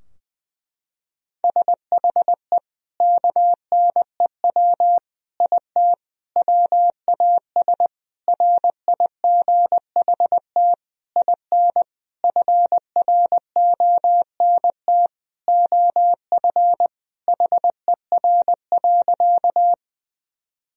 Morse code, heard as a keyed tone, about 21 wpm, 700 Hz. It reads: SHE KNEW IT WAS RIGHT IN FRONT OF HER.
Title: Sentences from Top 1000 Words - Review Part 2 - 20wpm